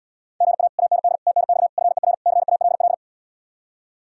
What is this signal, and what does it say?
Morse code, heard as a keyed tone, about 77 wpm, 700 Hz. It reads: GA AND TNX FER CALL